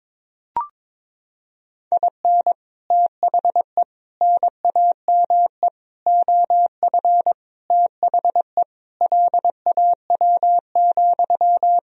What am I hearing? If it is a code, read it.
Morse code, heard as a keyed tone, about 22 wpm, 700 Hz. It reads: IN THE NAME OF THE LAW,
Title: Sentences from Top 1000 Words - Review Part 3 - 22wpm